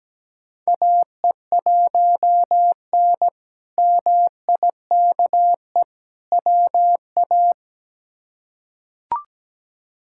Morse code, heard as a keyed tone, about 17 wpm, 700 Hz. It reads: AE1N MIKE WA
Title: K1USN SST QSO Practice - 17wpm